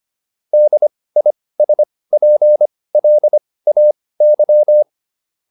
Morse code, heard as a keyed tone, about 25 wpm, spaced as for 18 wpm, 600 Hz. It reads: DISPLAY